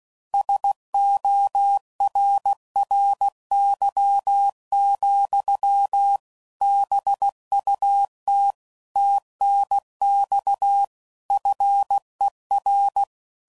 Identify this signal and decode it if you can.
Morse code, heard as a keyed tone, about 16 wpm, 800 Hz. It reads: SORRY, BUT TNX FER
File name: CW Code Groups @15WPM / QSOs / WA7QOX QRX